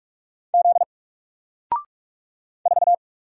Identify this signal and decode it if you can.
Morse code, heard as a keyed tone, about 44 wpm, 700 Hz. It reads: Z 4
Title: Single Letter-Number Alt 45wpm